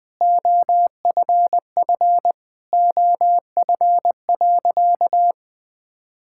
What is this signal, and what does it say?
Morse code, heard as a keyed tone, about 20 wpm, 700 Hz. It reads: OFF OF.
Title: Sentences from Top 1000 Words - Review Part 5 - 20wpm